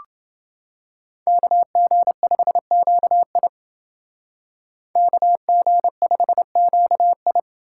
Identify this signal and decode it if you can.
Morse code, heard as a keyed tone, about 31 wpm, 700 Hz. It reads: KG5QI KG5QI